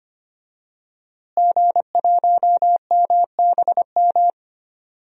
Morse code, heard as a keyed tone, about 25 wpm, 700 Hz. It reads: G1MBM
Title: DXCC-100 Call Signs - 5 Characters - Repeated 2x - 25wpm